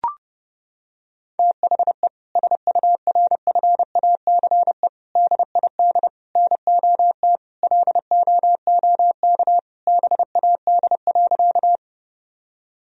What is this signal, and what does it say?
Morse code, heard as a keyed tone, about 32 wpm, 700 Hz. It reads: THE SURFACE DID NOT LOOK BAD.
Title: Sentences from Top 1000 Words - Review Part 3 - 30wpm